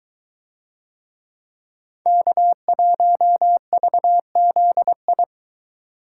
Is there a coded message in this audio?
Yes: Morse code from 2.1 to 5.2 s, heard as a keyed tone, about 22 wpm, 700 Hz. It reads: K1VZI